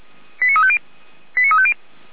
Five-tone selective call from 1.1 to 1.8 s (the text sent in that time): 89350